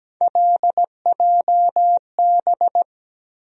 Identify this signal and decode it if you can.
Morse code, heard as a keyed tone, about 17 wpm, 700 Hz. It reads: LJB
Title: USA Call Sign Postfixes - 3 Letters 17wpm